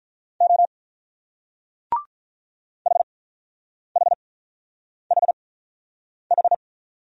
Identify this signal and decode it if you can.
Morse code, heard as a keyed tone, about 48 wpm, spaced as for 9 wpm, 700 Hz. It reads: O HHHH